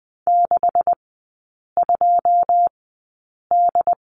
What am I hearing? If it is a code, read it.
Morse code, heard as a keyed tone, about 20 wpm, spaced as for 8 wpm, 700 Hz. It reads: 62D
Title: 138 Lesson 23e - Letter f - Three Characters 20wpm